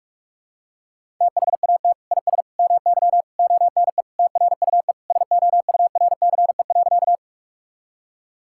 Morse code, heard as a keyed tone, about 46 wpm, 700 Hz. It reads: THAT IS MY ONE TRUE SOURCE.
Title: Sentences from Top 1000 Words - Review Part 5 - 45wpm